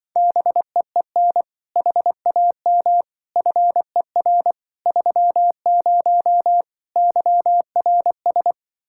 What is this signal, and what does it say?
Morse code, heard as a keyed tone, about 24 wpm, 700 Hz. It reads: BEEN HAM FER 30 YRS
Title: Speed-Racing Common QSO Phrases - 20wpm